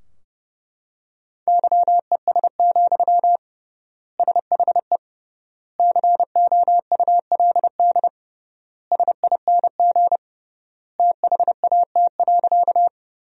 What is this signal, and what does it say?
Morse code, heard as a keyed tone, about 30 wpm, 700 Hz. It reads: YES, SHE COULD SING THAT.
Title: 3x Word-Spacing - Sentences from Top 500 Words - 30wpm